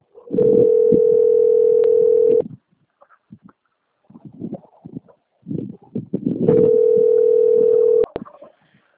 Ringback tone (North America), 440 + 480 Hz.
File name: out- - - - - .25687